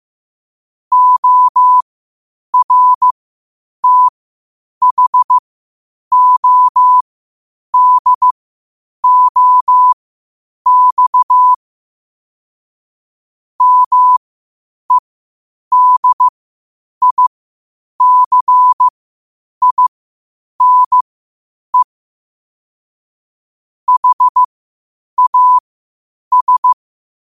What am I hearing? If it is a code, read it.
Morse code, heard as a keyed tone, about 15 wpm, spaced as for 8 wpm, 1000 Hz. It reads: ORTHODOX MEDICINE HAS
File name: qotdfiles / 8wpm